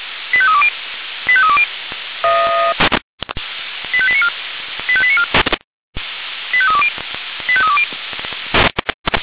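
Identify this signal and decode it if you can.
Five-tone selective call (ZVEI), heard as quick runs of five tones, tones of about 70 ms. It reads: 85320 85320 85894 85894 85320 85320
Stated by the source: ZVEI five-tone selective call received by SDR, from the Feuerwehr Orlamünde fire brigade audio server.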